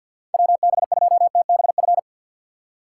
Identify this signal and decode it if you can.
Morse code, heard as a keyed tone, about 53 wpm, 700 Hz. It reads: WB2TBF
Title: USA Call Signs - 6 Characters - 4x Character Spacing - 50wpm